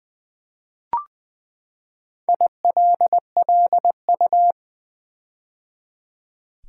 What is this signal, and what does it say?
Morse code, heard as a keyed tone, about 20 wpm, 700 Hz. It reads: ILLU
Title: Build A Word - 4 Letters At A Time - 12-Letter Words - 20wpm